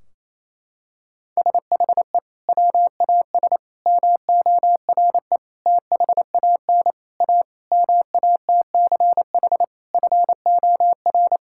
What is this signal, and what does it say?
Morse code, heard as a keyed tone, about 28 wpm, 700 Hz. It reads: SHE WAS MORE THAN A MATCH FOR
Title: Sentences from Top 1600 Words 28wpm